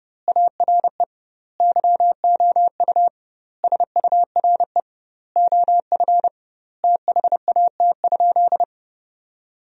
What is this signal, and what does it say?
Morse code, heard as a keyed tone, about 30 wpm, 700 Hz. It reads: ARE YOU SURE OF THAT?
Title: 2x Word-Spacing - Sentences from Top 300 Words - 30wpm